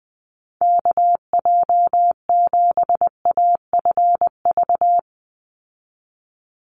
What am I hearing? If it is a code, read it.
Morse code, heard as a keyed tone, about 20 wpm, 700 Hz. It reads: KJ7AFV
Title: USA Call Signs 20wpm